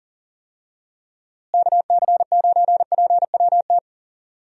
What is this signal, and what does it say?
Morse code, heard as a keyed tone, about 37 wpm, 700 Hz. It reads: KC9PWT
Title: USA Call Signs - 6 Characters - Repeated 2x - 40wpm